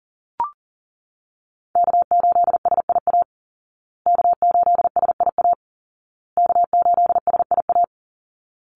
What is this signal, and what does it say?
Morse code, heard as a keyed tone, about 40 wpm, 700 Hz. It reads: K8SIA K8SIA K8SIA
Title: Call Signs Repeated 2x - 40wpm